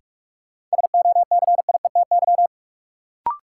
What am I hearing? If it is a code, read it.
Morse code, heard as a keyed tone, about 46 wpm, 700 Hz. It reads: SOCIETY
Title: Sentences from Top 700 Words - New Word Drill - 45wpm